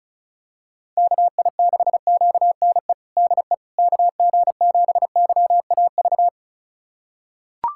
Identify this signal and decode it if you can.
Morse code, heard as a keyed tone, about 36 wpm, 690 Hz. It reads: KI6QNE DE KG7YAV
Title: USA Callsign DE Callsign 35wpm